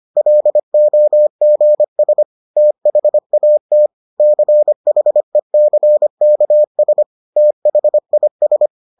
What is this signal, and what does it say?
Morse code, heard as a keyed tone, about 25 wpm, 600 Hz. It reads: LOGS THAT CHECKS THIS